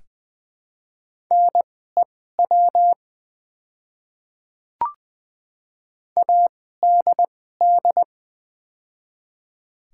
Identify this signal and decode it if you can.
Morse code, heard as a keyed tone, about 20 wpm, spaced as for 15 wpm, 700 Hz. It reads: NEW ADD
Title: 2x Character Spacing - Top 200 Words - 20wpm